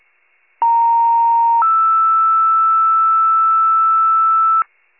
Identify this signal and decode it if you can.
Two-tone page, heard as a one-second tone, then a three-second tone, about 930 then 1320 Hz, with noise. It is heard from 0.6 to 4.6 s.